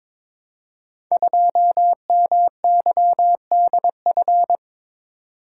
Morse code, heard as a keyed tone, about 22 wpm, 700 Hz. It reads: 2MYDF
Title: Code Groups - 5 Letter:Numbers - 22wpm